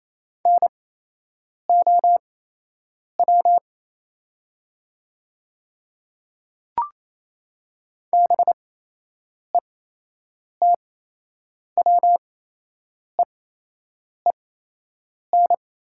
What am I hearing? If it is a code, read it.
Morse code, heard as a keyed tone, about 28 wpm, spaced as for 8 wpm, 700 Hz. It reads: NOW BETWEEN
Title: 8x Character Spacing - Top 300 Words - 28wpm